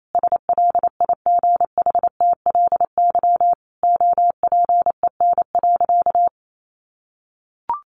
Morse code, heard as a keyed tone, about 28 wpm, 700 Hz. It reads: SLIGHTLY OPEN.
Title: Sentences from Top 1300 Words 28wpm